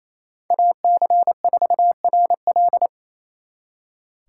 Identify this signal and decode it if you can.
Morse code, heard as a keyed tone, about 36 wpm, 700 Hz. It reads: AC4RL